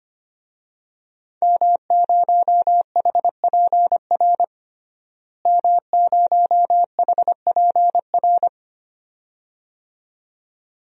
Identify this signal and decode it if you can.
Morse code, heard as a keyed tone, about 25 wpm, 700 Hz. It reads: M0HPR M0HPR
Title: DXCC-100 Call Signs - 5 Characters - Repeated 2x - 25wpm